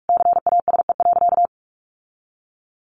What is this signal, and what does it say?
Morse code, heard as a keyed tone, about 44 wpm, 700 Hz. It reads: CASE.
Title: Sentences from Top 1300 Words 45wpm